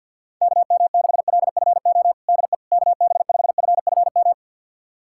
Morse code, heard as a keyed tone, about 46 wpm, 700 Hz. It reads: KM6LFO DE KB5FFM
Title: USA Callsign DE Callsign - 6 Characters - 50wpm